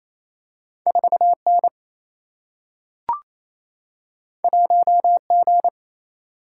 Morse code, heard as a keyed tone, about 28 wpm, 700 Hz. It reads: 4 N 1 G
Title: Single-Letter Single-Number 28wpm